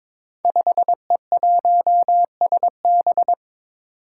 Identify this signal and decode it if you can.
Morse code, heard as a keyed tone, about 22 wpm, 700 Hz. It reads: 5E1SB
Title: Five Challenging Letter-Number Combinations 22wpm